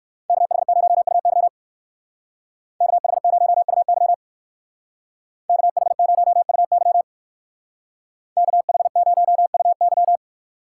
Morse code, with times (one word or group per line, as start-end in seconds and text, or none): 0.3-1.5 KH0UY
2.8-4.1 KH0UY
5.5-7.0 KH0UY
8.4-10.2 KH0UY